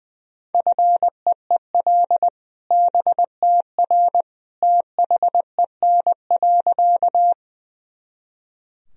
Morse code, heard as a keyed tone, about 20 wpm, 700 Hz. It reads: FEEL BTR THEN.
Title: Sentences from Top 400 Words and Abbreviations 20wpm